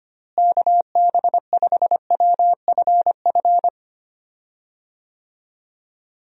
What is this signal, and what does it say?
Morse code, heard as a keyed tone, about 25 wpm, 700 Hz. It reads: KB5WFF
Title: USA Call Signs - 6 Characters - Repeated 3x - 25wpm